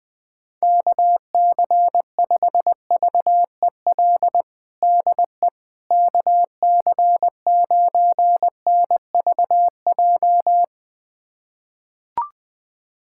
Morse code, heard as a keyed tone, about 20 wpm, 700 Hz. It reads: KC5VEL DE KC9NVJ
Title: USA Callsign DE Callsign - 6 Characters - 20wpm